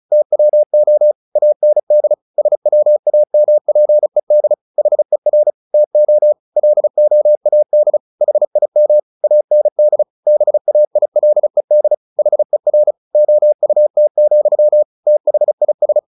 Morse code, heard as a keyed tone, about 34 wpm, 600 Hz. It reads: TWO AND SWAMPED HER TO LOAD HIM AND BAILED HER OUT, THIS
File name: the old man and the sea 2 35wpm